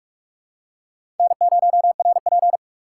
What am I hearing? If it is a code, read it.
Morse code, heard as a keyed tone, about 41 wpm, 700 Hz. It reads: N0RP
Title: USA Call Signs - 2x Character Spacing - 45wpm